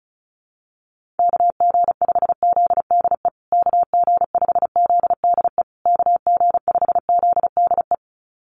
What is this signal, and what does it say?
Morse code, heard as a keyed tone, about 34 wpm, 700 Hz. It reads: KG5ZDE KG5ZDE KG5ZDE
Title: USA Call Signs Repeated 3x 35wpm